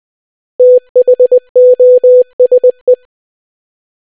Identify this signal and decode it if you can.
Morse code, heard as a keyed tone, about 20 wpm, 500 Hz. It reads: THOSE